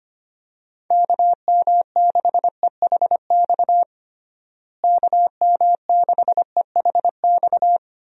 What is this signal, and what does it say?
Morse code, heard as a keyed tone, about 25 wpm, 700 Hz. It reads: KM6EHX KM6EHX
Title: USA Call Signs - 6 Characters - Repeated 2x - 25wpm